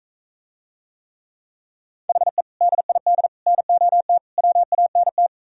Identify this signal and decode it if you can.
Morse code, heard as a keyed tone, about 45 wpm, 700 Hz. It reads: HE DID NOT WANT